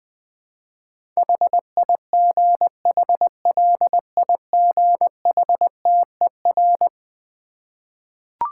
Morse code, heard as a keyed tone, about 20 wpm, 700 Hz. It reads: HIGHLIGHTER
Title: Difficult Words with Hints - 20wpm